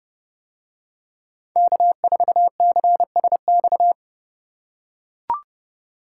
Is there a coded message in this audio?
Yes: Morse code from 1.6 to 3.9 s, heard as a keyed tone, about 30 wpm, 700 Hz. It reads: K4CSX